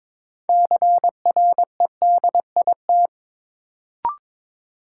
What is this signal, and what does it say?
Morse code, heard as a keyed tone, about 22 wpm, 700 Hz. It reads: CREDIT
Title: Sentences from Top 1000 Words - New Word Drill - 22wpm